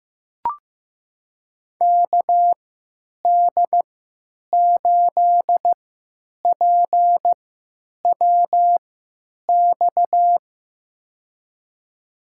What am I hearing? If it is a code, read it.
Morse code, heard as a keyed tone, about 15 wpm, spaced as for 9 wpm, 700 Hz. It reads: KD8PWX
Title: USA Call Signs - 3x Character Spacing - 15wpm